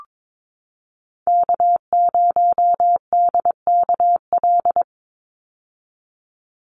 Morse code, heard as a keyed tone, about 22 wpm, 700 Hz. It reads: K0DKL
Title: USA Call Signs 22wpm